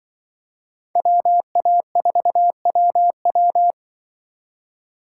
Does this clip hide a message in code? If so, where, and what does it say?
Yes: Morse code from 1.0 to 3.7 s, heard as a keyed tone, about 25 wpm, 700 Hz. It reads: WA4WW